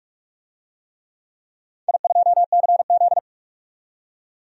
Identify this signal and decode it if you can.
Morse code, heard as a keyed tone, about 44 wpm, 700 Hz. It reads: I2CZ
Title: DXCC-100 Call Signs - 4 Characters - 45wpm